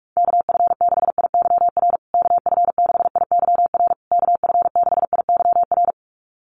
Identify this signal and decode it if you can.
Morse code, heard as a keyed tone, about 44 wpm, 700 Hz. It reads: KF6IYR KF6IYR KF6IYR
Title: USA Call Signs Repeated 3x 45wpm